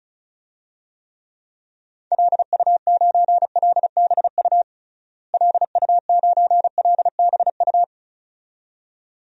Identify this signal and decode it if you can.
Morse code, heard as a keyed tone, about 36 wpm, 700 Hz. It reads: LU9LBU LU9LBU
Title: DXCC-100 Call Signs - 6 Characters - Repeated 2x - 35wpm